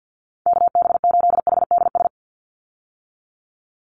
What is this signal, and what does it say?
Morse code, heard as a keyed tone, about 48 wpm, 700 Hz. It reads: KB8HDS
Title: USA Call Signs 50wpm